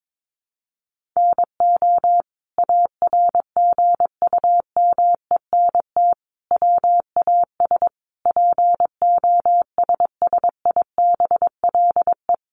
Morse code, heard as a keyed tone, about 22 wpm, 700 Hz. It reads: NO ARGUMENT WAS POSSIBLE
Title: Sentences from Top 1400 Words 22wpm